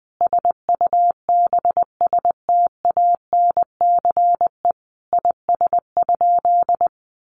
Morse code, heard as a keyed tone, about 20 wpm, 700 Hz. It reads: SUBSTANCE IS?
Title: Sentences from Top 1500 Words 20wpm